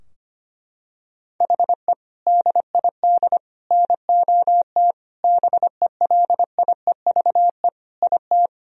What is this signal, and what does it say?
Morse code, heard as a keyed tone, about 25 wpm, 700 Hz. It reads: HE DID NOT BELIEVE IT
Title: Sentences from Top 1000 Words - Review Part 2 - 25wpm